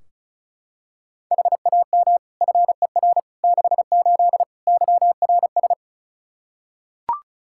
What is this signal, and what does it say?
Morse code, heard as a keyed tone, about 36 wpm, 700 Hz. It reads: HAM FER 68 YRS
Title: K1USN SST QSO Practice 35wpm